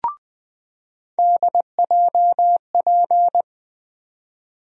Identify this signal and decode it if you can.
Morse code, heard as a keyed tone, about 20 wpm, 700 Hz. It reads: DJP
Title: USA Call Sign Postfixes - 3 Letters 20wpm